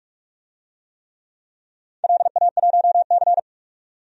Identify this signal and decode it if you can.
Morse code, heard as a keyed tone, about 47 wpm, 700 Hz. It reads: LA1C
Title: DXCC-100 Call Signs - 4 Characters - 45wpm